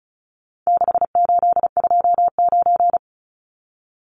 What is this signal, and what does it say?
Morse code, heard as a keyed tone, about 34 wpm, 700 Hz. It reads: 6829
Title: Numbers - 4-Digit 35wpm